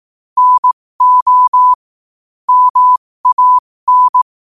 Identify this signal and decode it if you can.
Morse code, heard as a keyed tone, about 18 wpm, spaced as for 15 wpm, 1000 Hz. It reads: NO MAN
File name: qotdfiles / 15wpm